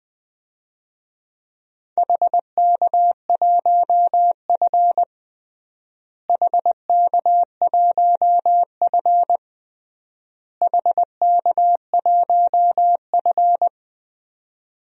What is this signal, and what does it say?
Morse code, heard as a keyed tone, about 20 wpm, 700 Hz. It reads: HK1F HK1F HK1F
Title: DXCC-100 Call Signs - 4 Characters - Repeated 3x - 20wpm